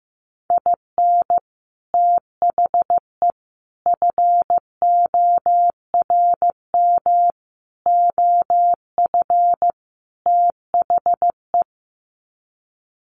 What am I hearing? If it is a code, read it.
Morse code, heard as a keyed tone, about 15 wpm, 700 Hz. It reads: IN THE FORM OF THE
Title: Sets of 5 Words Encore2 15wpm